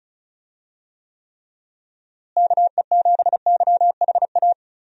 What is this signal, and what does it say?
Morse code, heard as a keyed tone, about 36 wpm, 700 Hz. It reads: KE7YHA
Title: USA Call Signs - 6 Characters - Repeated 3x - 35wpm